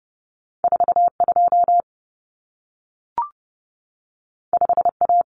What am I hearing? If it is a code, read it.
Morse code, heard as a keyed tone, about 30 wpm, 700 Hz. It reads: 42 5A
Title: Common Cut Numbers - 2-Digit 30wpm